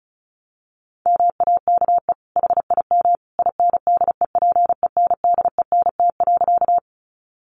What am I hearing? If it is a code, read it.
Morse code, heard as a keyed tone, about 34 wpm, 700 Hz. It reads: MAKE HIM INDEPENDENT.
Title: Sentences from Top 900 Words 35wpm